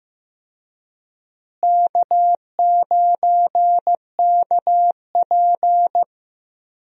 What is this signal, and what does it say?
Morse code, heard as a keyed tone, about 15 wpm, 700 Hz. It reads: K9KP
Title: USA Call Signs - 4 Characters - Repeated 2x - 15wpm